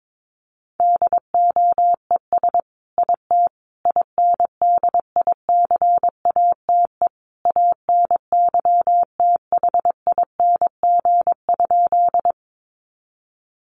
Morse code, heard as a keyed tone, about 22 wpm, 700 Hz. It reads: DOES IT INDICATE ANYTHING?
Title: Sentences from Top 1300 Words 22wpm